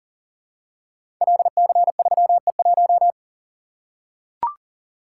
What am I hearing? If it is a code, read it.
Morse code, heard as a keyed tone, about 40 wpm, 700 Hz. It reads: LC3E1
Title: Code Groups - 5 Letter:Numbers - 40wpm